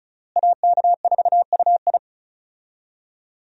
Morse code, heard as a keyed tone, about 36 wpm, 700 Hz. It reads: AK4UI
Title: USA Call Signs - 5 Characters - Repeated 3x - 35wpm